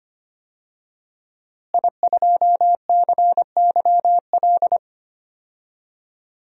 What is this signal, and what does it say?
Morse code, heard as a keyed tone, about 25 wpm, 700 Hz. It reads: I2CYL
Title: DXCC-100 Call Signs - 5 Characters - Repeated 3x - 25wpm